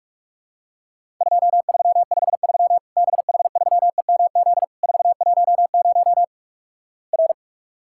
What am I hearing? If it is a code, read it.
Morse code, heard as a keyed tone, about 44 wpm, 700 Hz. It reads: 2353 BH3EMZ 410 R
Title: Worked All Europe DX Contest - QTC Exchange - 45wpm